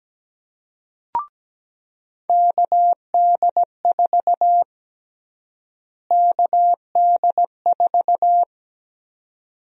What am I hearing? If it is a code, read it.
Morse code, heard as a keyed tone, about 17 wpm, 700 Hz. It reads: KD4 KD4
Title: USA Call Sign Prefixes - Repeated 2x - 17wpm